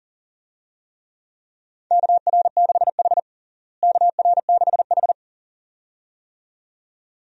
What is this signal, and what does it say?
Morse code, heard as a keyed tone, about 40 wpm, 700 Hz. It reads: KR6H KR6H
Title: USA Call Signs - 4 Characters - Repeated 2x - 40wpm